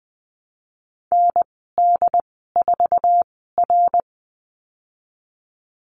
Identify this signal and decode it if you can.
Morse code, heard as a keyed tone, about 20 wpm, spaced as for 14 wpm, 700 Hz. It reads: ND4R
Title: 088 Lesson 15g - Number 5 - Call Signs 20wpm